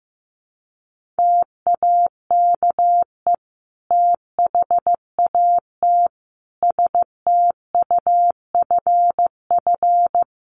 Morse code, heard as a keyed tone, about 15 wpm, 700 Hz. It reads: TAKE THAT STUFF
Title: Sentences from Top 1400 Words 15wpm